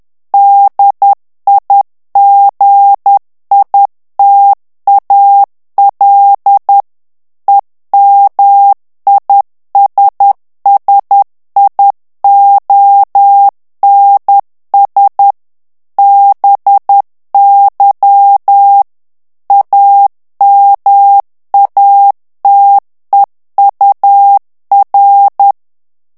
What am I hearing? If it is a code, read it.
Morse code, heard as a keyed tone, about 10 wpm, 800 Hz. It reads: DIGITAL EMISSIONS BY AMATEUR
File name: ANGLAIS texte clair 3 vitesse 10 mots minute